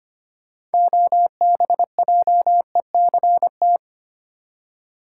Morse code, heard as a keyed tone, about 25 wpm, 700 Hz. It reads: OBJECT
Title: Six Letter Words - 25wpm